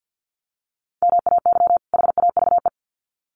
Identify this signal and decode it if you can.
Morse code, heard as a keyed tone, about 48 wpm, 700 Hz. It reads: MAY HAVE
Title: Sets of 2 Words 50wpm